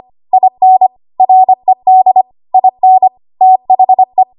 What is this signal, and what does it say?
Morse code, heard as a keyed tone, about 25 wpm, 750 Hz. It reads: IN RED IN THE
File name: Archive / 25 WPM / 25WPM